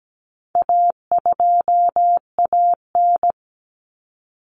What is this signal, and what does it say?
Morse code, heard as a keyed tone, about 17 wpm, 700 Hz. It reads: A2AN
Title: Common Cut Numbers - 4-Digit 17wpm